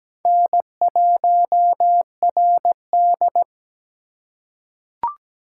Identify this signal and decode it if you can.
Morse code, heard as a keyed tone, about 17 wpm, 700 Hz. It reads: N1RD
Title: USA Call Signs - 4 Characters - 4x Character Spacing - 17wpm